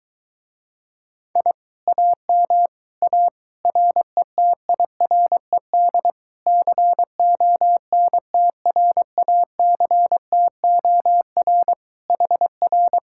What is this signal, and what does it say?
Morse code, heard as a keyed tone, about 22 wpm, 700 Hz. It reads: I AM A RETIRED CONTRACTOR HR